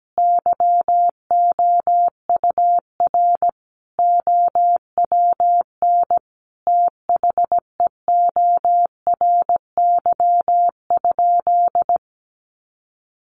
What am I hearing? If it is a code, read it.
Morse code, heard as a keyed tone, about 17 wpm, 700 Hz. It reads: YOUR OWN THEORY?
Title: Sentences from Top 1000 Words 17wpm